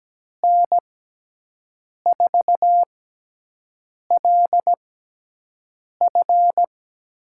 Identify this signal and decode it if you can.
Morse code, heard as a keyed tone, about 17 wpm, spaced as for 6 wpm, 700 Hz. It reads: N4LF
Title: USA Call Signs - 6 Characters - 6x Character Spacing - 17wpm